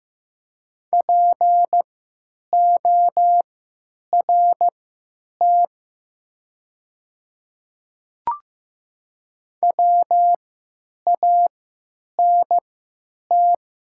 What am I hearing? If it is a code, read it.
Morse code, heard as a keyed tone, about 15 wpm, spaced as for 9 wpm, 700 Hz. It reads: PORT WANT
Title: 3x Character Spacing - Top 300 Words - 15wpm